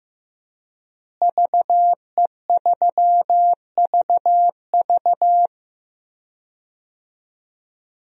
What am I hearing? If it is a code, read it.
Morse code, heard as a keyed tone, about 15 wpm, 700 Hz. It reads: VE3VV
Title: DXCC-100 Call Signs - 5 Characters - Repeated 3x - 15wpm